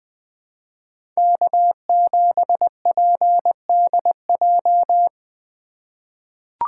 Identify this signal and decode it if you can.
Morse code, heard as a keyed tone, about 20 wpm, 700 Hz. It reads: K7PDJ
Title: USA Call Signs - 5 Characters - 3x Character Spacing - 20wpm